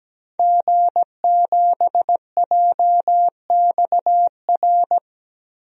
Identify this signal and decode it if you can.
Morse code, heard as a keyed tone, about 17 wpm, 700 Hz. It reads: G7JXR
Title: DXCC-100 Call Signs - 5 Characters - 17wpm